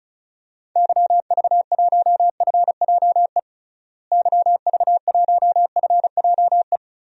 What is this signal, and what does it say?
Morse code, heard as a keyed tone, about 34 wpm, 700 Hz. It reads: YV1FJE YV1FJE
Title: DXCC-100 Call Signs - 6 Characters - Repeated 3x - 35wpm